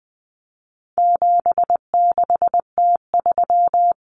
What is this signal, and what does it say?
Morse code, heard as a keyed tone, about 20 wpm, 700 Hz. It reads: 76T3
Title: Common Cut Numbers - 4-Digit 20wpm